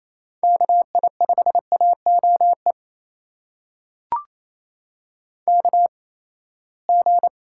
Morse code, heard as a keyed tone, about 28 wpm, spaced as for 19 wpm, 700 Hz. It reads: KI5AOE KG
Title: USA Call Signs - 8x Character Spacing - 28wpm